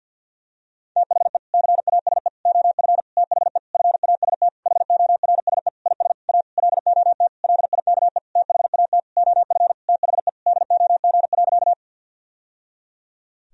Morse code, heard as a keyed tone, about 50 wpm, 700 Hz. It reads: THE CASE OF THE FAST HORSE IS A LOT LIKE THAT OF THE DOG.
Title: Sentences from Top 1000 Words - Review Part 2 - 50wpm